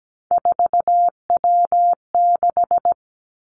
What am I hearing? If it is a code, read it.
Morse code, heard as a keyed tone, about 17 wpm, 700 Hz. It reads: 4W6
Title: Three Challenging Letter-Number Combinations 17wpm